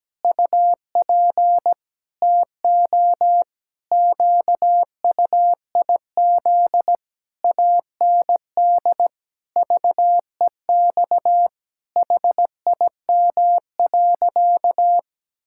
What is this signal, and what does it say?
Morse code, heard as a keyed tone, about 17 wpm, 700 Hz. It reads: UP TO QUIZ AND VEX HIM.
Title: Pangram Challenge v2 - 17wpm